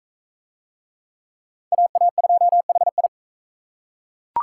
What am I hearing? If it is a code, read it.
Morse code, heard as a keyed tone, about 39 wpm, 700 Hz. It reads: AA2HI